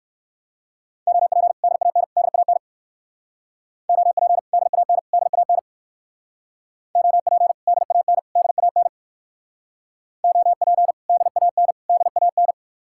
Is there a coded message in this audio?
Yes: Morse code from 1.1 to 12.5 s, heard as a keyed tone, about 64 wpm, 700 Hz. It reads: OP DAN DAN OP DAN DAN OP DAN DAN OP DAN DAN